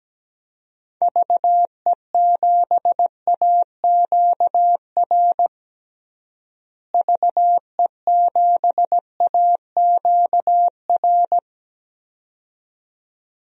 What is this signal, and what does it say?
Morse code, heard as a keyed tone, about 17 wpm, 700 Hz. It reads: VE7AQR VE7AQR
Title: DXCC-100 Call Signs - 6 Characters - Repeated 2x - 17wpm